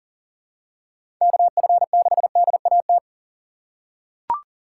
Morse code, heard as a keyed tone, about 40 wpm, 700 Hz. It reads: KF6DAT
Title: USA Call Signs - 6 Characters - 4x Character Spacing - 40wpm